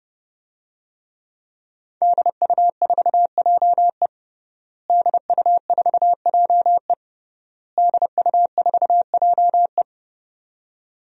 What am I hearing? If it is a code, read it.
Morse code, heard as a keyed tone, about 30 wpm, 700 Hz. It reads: DU4JE DU4JE DU4JE
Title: DXCC-100 Call Signs - 5 Characters - Repeated 3x - 30wpm